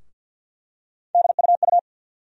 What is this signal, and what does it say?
Morse code, heard as a keyed tone, about 52 wpm, 690 Hz. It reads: DUU
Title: USA Call Sign Postfixes - 3 Letters 50wpm